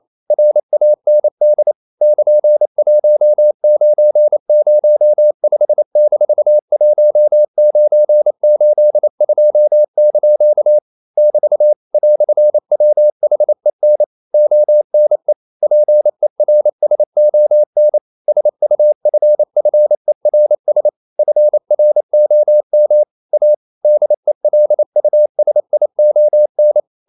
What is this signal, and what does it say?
Morse code, heard as a keyed tone, about 28 wpm, 600 Hz. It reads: RAND <KN>1905-1982) <BT> "WHEN ONE PERSON SUFFERS FROM A DELUSION
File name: CW / Practice / EnglishQuotations0128